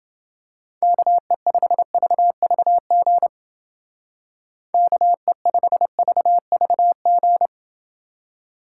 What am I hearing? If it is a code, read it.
Morse code, heard as a keyed tone, about 28 wpm, 700 Hz. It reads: KE5VVG KE5VVG